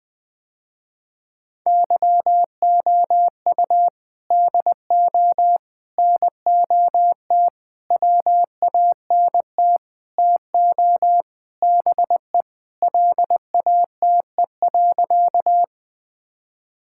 Morse code, heard as a keyed tone, about 20 wpm, 700 Hz. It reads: YOU DO NOT WANT TO BE LATE.
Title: Sentences with 5-Word Phrases - Spoken Beginning - 20wpm